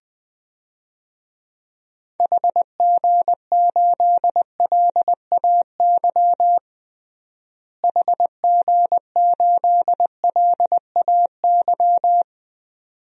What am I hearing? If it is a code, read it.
Morse code, heard as a keyed tone, about 20 wpm, 700 Hz. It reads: HG8LAY HG8LAY
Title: DXCC-100 Call Signs - 6 Characters - Repeated 2x - 20wpm